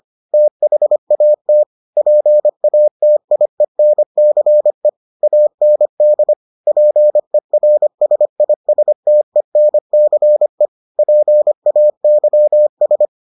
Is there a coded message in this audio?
Yes: Morse code from 0.3 to 13.1 s, heard as a keyed tone, about 25 wpm, 600 Hz. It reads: THAT PATIENCE AND PERSISTENCE PAYS